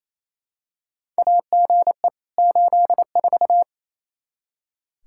Morse code, heard as a keyed tone, about 28 wpm, 700 Hz. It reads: AGE 84
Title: Practice QSO - Exchange 3 - More Info - 28wpm